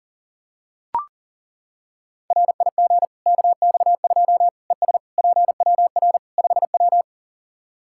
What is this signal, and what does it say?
Morse code, heard as a keyed tone, about 40 wpm, 700 Hz. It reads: RIG KX2 ES PWR 5W
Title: K1USN SST QSO Practice 40wpm